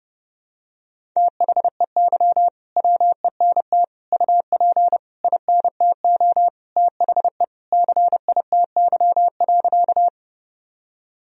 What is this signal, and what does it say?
Morse code, heard as a keyed tone, about 29 wpm, 700 Hz. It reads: THEY WENT UP INTO THE CITY.
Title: Sentences from Top 1000 Words - Review Part 1 - 30wpm